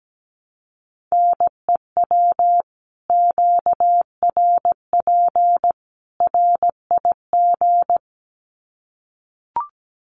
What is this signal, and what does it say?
Morse code, heard as a keyed tone, about 17 wpm, 700 Hz. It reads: NEW QRP RIG
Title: Common QSO Phrases 17wpm